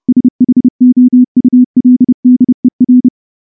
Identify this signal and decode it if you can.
Morse code, heard as a keyed tone, about 28 wpm, 250 Hz. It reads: SHOULDER
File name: morse-training-30WPM